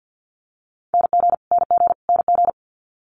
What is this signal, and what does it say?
Morse code, heard as a keyed tone, about 50 wpm, 700 Hz. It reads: NG NG NG
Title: Multi-letter Phonemes Repeated 3x 50wpm